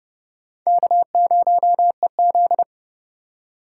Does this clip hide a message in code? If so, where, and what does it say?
Yes: Morse code from 0.7 to 2.6 s, heard as a keyed tone, about 28 wpm, 700 Hz. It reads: K0EZ